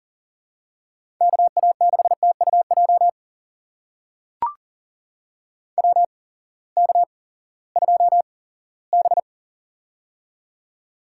Morse code, heard as a keyed tone, about 40 wpm, spaced as for 20 wpm, 700 Hz. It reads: KA6TUJ WK2B
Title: USA Call Signs - 8x Character Spacing - 40wpm